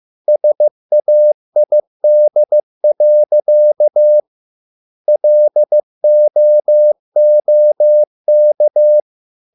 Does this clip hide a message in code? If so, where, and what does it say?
Yes: Morse code from 0.3 to 9.0 s, heard as a keyed tone, about 15 wpm, 600 Hz. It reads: SAID. LOOK